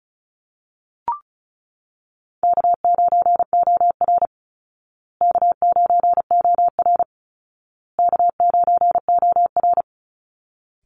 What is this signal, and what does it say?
Morse code, heard as a keyed tone, about 34 wpm, 700 Hz. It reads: K9OR K9OR K9OR
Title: Call Signs Repeated 2x - 35wpm